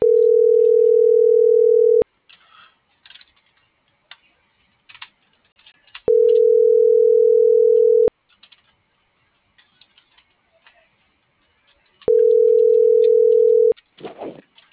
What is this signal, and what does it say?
Ringback tone (North America), 440 + 480 Hz.